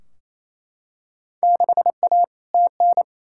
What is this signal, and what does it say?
Morse code, heard as a keyed tone, about 28 wpm, 700 Hz. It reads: 6A TN
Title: Field Day Exchanges Encore - 28wpm